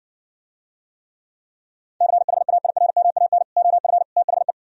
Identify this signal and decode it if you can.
Morse code, heard as a keyed tone, about 68 wpm, 700 Hz. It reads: CHAIRMAN OF THE